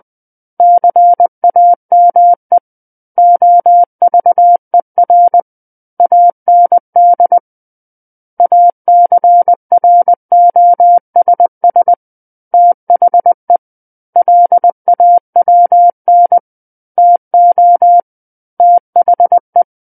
Morse code, heard as a keyed tone, about 20 wpm, 700 Hz. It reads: CAME OVER AND ACROSS THE LAWN TO THE
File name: 20 WPM